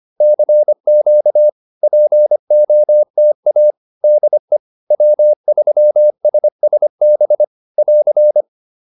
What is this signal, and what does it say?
Morse code, heard as a keyed tone, about 25 wpm, 600 Hz. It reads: CQ POTA DE W3SSB <AR>